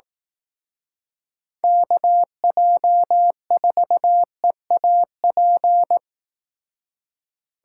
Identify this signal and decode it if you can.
Morse code, heard as a keyed tone, about 18 wpm, 700 Hz. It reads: KJ4EAP
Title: Speed Racing - USA 2x3 Call Signs 15wpm